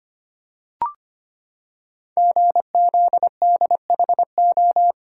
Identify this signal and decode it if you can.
Morse code, heard as a keyed tone, about 25 wpm, 700 Hz. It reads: GZDHO
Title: Code Groups - 5 Letters - 25wpm